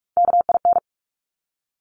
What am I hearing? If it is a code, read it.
Morse code, heard as a keyed tone, about 46 wpm, 700 Hz. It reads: KIN
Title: Top Three Letter Combinations in Words 45wpm